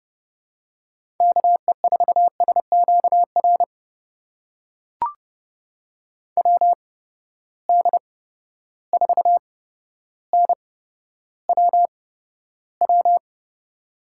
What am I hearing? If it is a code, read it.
Morse code, heard as a keyed tone, about 32 wpm, spaced as for 13 wpm, 700 Hz. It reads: KE4SQR WD4NWW
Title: USA Call Signs - 6 Characters - 8x Character Spacing - 30wpm